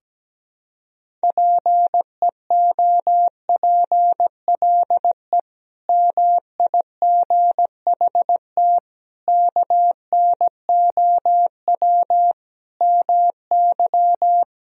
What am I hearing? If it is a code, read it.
Morse code, heard as a keyed tone, about 17 wpm, 700 Hz. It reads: PEOPLE MIGHT KNOW MY
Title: Sentences from Top 500 Words and Abbreviations 17wpm